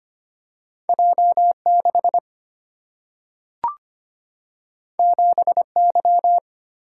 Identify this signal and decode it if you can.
Morse code, heard as a keyed tone, about 25 wpm, 700 Hz. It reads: J6 7Y
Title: Single-Letter Single-Number 25wpm